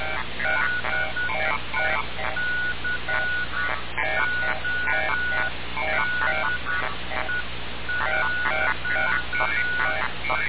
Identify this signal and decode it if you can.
Morse code, heard as a keyed tone, about 10 wpm, 1460 Hz. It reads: DECODE.
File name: radiocap / REC z